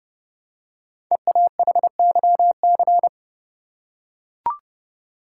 Morse code, heard as a keyed tone, about 30 wpm, 700 Hz. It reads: EAHYC
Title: Code Groups - 5 Letters - 30wpm